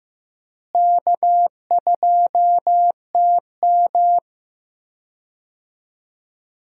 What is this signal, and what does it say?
Morse code, heard as a keyed tone, about 16 wpm, 700 Hz. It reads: K2TM
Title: USA Call Signs - 4 Characters - Repeated 2x - 15wpm